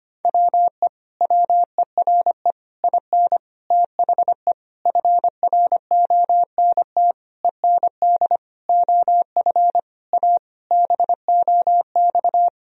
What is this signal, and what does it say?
Morse code, heard as a keyed tone, about 25 wpm, 700 Hz. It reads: WE WERE IN THE FRONT END OF A BOX
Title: Sentences from Top 1000 Words - Review Part 2 - 25wpm